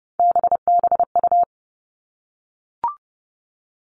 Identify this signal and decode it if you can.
Morse code, heard as a keyed tone, about 30 wpm, 700 Hz. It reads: BBU
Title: All Cut Numbers - 3-Digit 30wpm